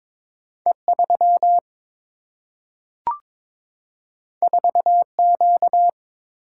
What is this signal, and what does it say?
Morse code, heard as a keyed tone, about 22 wpm, 700 Hz. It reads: E3 4Q
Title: Single-Letter Single-Number 22wpm